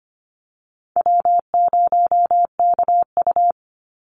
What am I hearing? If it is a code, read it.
Morse code, heard as a keyed tone, about 25 wpm, 700 Hz. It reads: W0KU
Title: Easy Call Signs - 25wpm